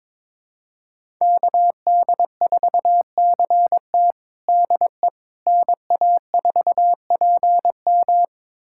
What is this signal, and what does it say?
Morse code, heard as a keyed tone, about 22 wpm, 700 Hz. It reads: KD4CT DE NA4PM
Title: USA Callsign DE Callsign - 5 Characters - 22wpm